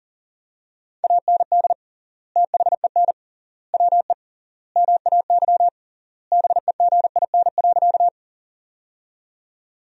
Morse code, heard as a keyed tone, about 40 wpm, 700 Hz. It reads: AND THEN WE MAY BEGIN.
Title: 3x Word-Spacing - Sentences from Top 300 Words - 40wpm